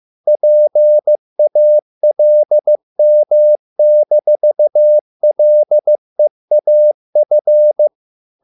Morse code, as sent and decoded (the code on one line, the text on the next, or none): .--. .- .-.. -- -....- .-.. .- ..-.
PALM-LEAF